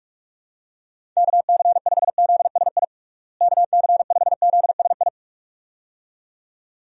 Morse code, heard as a keyed tone, about 41 wpm, 700 Hz. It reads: KC5ZSI KC5ZSI
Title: USA Call Signs - 6 Characters - Repeated 2x - 45wpm